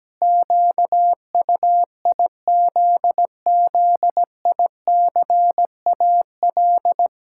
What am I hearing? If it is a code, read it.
Morse code, heard as a keyed tone, about 17 wpm, 700 Hz. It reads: QUIZZICAL
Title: Pangram Challenge v2 - 17wpm